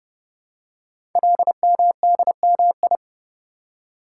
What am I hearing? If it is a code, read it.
Morse code, heard as a keyed tone, about 30 wpm, 700 Hz. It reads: LMDMI
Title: Code Groups - 5 Letters - 30wpm